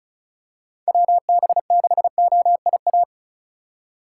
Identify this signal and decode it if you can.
Morse code, heard as a keyed tone, about 36 wpm, 700 Hz. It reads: WB6OIA
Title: USA Call Signs - 6 Characters - 2x Character Spacing - 35wpm